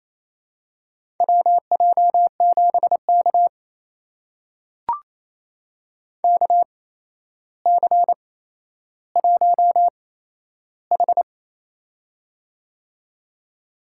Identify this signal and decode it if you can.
Morse code, heard as a keyed tone, about 28 wpm, spaced as for 12 wpm, 700 Hz. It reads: WJ7K KC1H
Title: USA Call Signs - 4 Characters - 8x Character Spacing - 28wpm